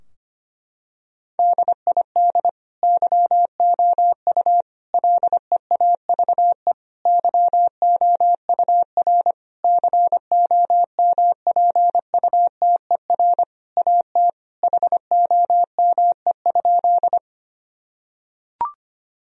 Morse code, heard as a keyed tone, about 25 wpm, 700 Hz. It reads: DID YOU LEAVE YOUR COMPUTER AT HOME?
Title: Sentences from Top 1000 Words - Review Part 3 - 25wpm